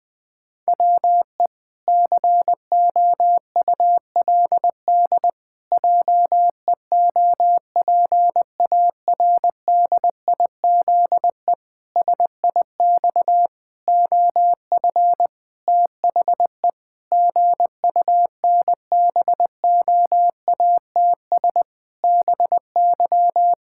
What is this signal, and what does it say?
Morse code, heard as a keyed tone, about 21 wpm, 700 Hz. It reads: WE COULD JEOPARDIZE SIX OF THE GUNBOATS BY
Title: Pangram Challenge v2 - 20wpm